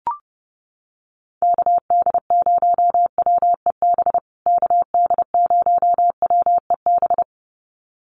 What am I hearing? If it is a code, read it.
Morse code, heard as a keyed tone, about 30 wpm, 700 Hz. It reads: KD0WEB KD0WEB
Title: USA Call Signs Repeated 2x 30wpm